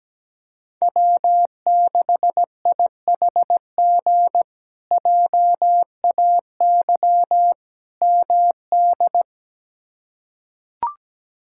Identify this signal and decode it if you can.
Morse code, heard as a keyed tone, about 17 wpm, 700 Hz. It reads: W6IHG JAY MD
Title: K1USN SST QSO Practice - 17wpm